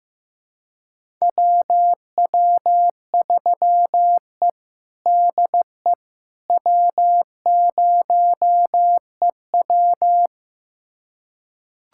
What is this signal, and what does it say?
Morse code, heard as a keyed tone, about 15 wpm, 700 Hz. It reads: WW3E DE W0EW
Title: USA Callsign DE Callsign - 4 Characters - 15wpm